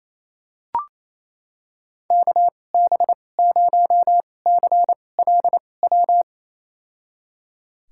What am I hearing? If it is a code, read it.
Morse code, heard as a keyed tone, about 28 wpm, spaced as for 20 wpm, 700 Hz. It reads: KB0CLW
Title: USA Call Signs - 2x Character Spacing - 28wpm